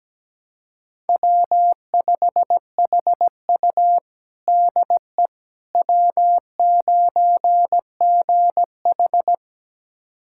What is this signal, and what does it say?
Morse code, heard as a keyed tone, about 17 wpm, 700 Hz. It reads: W5HU DE W9GH
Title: USA Callsign DE Callsign - 4 Characters - 17wpm